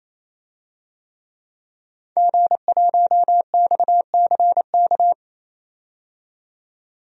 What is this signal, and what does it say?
Morse code, heard as a keyed tone, about 27 wpm, 700 Hz. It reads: G1XCK
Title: DXCC-100 Call Signs - 5 Characters - 28wpm